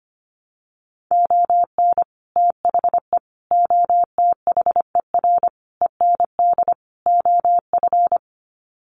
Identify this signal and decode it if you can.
Morse code, heard as a keyed tone, about 25 wpm, 700 Hz. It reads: ON THE OTHER END OF
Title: Sets of 5 Words Encore 25wpm